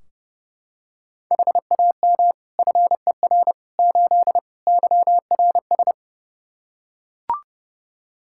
Morse code, heard as a keyed tone, about 32 wpm, 700 Hz. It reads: HAM FER 8 YRS
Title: Practice QSO - Exchange 3 - More Info - 30wpm